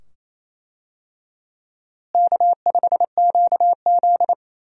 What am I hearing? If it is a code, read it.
Morse code, heard as a keyed tone, about 28 wpm, 700 Hz. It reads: K5QZ
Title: USA Call Signs - 4 Characters - Repeated 2x - 28wpm